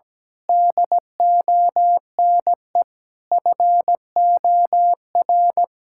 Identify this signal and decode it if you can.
Morse code, heard as a keyed tone, about 17 wpm, 700 Hz. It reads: DONE FOR
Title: Sentences from Top 1600 Words 17wpm